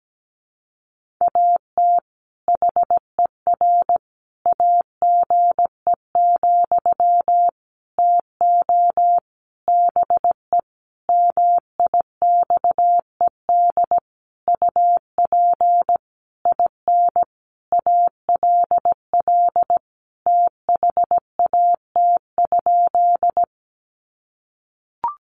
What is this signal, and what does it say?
Morse code, heard as a keyed tone, about 17 wpm, 700 Hz. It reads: AT HER AGE, TO BE MIXED UP IN ALL THAT?
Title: Sentences from Top 1500 Words 17wpm